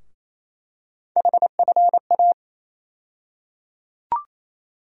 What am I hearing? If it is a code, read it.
Morse code, heard as a keyed tone, about 28 wpm, 700 Hz. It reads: HFA
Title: USA Call Sign Postfixes - 3 Letters 28wpm